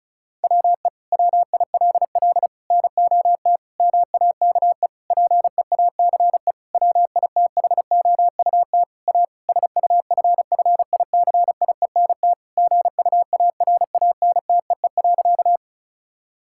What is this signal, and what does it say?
Morse code, heard as a keyed tone, about 36 wpm, 700 Hz. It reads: WE WILL NOT MAKE PEACE WITHOUT A SUFFICIENT GUARANTEE.
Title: Sentences from Top 1600 Words 35wpm